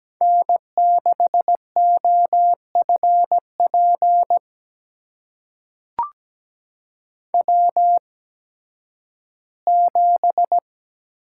Morse code, heard as a keyed tone, about 17 wpm, spaced as for 11 wpm, 700 Hz. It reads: N6OFP W7
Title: USA Call Signs - 8x Character Spacing - 17wpm